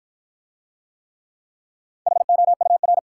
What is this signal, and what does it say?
Morse code, heard as a keyed tone, about 54 wpm, 700 Hz. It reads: HOUR